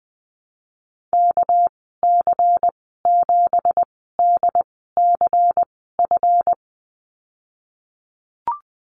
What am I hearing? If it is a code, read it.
Morse code, heard as a keyed tone, about 20 wpm, spaced as for 15 wpm, 700 Hz. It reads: KC7DCF
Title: 213 Lesson 34g - Number 8 - Call Signs 20wpm